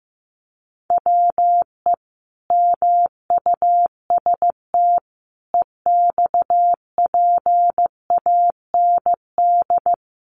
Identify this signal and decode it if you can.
Morse code, heard as a keyed tone, about 15 wpm, 700 Hz. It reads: WE MUST EXPAND
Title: Sentences from Top 1400 Words 15wpm